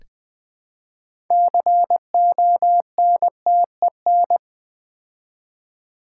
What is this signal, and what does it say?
Morse code, heard as a keyed tone, about 20 wpm, 700 Hz. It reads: CONTEN
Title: Build A Word - 3 Letters At A Time - 9-Letter Words - 20wpm